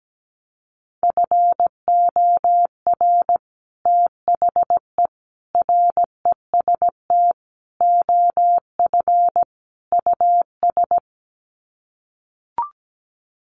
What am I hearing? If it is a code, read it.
Morse code, heard as a keyed tone, about 17 wpm, 700 Hz. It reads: FOR THE REST OF US
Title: Sets of 5 Words Encore 17wpm